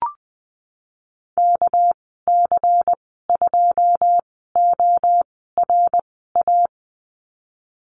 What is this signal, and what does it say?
Morse code, heard as a keyed tone, about 20 wpm, spaced as for 15 wpm, 700 Hz. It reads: KC2ORA
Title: 231 Lesson 37g - Letter x - Call Signs 20wpm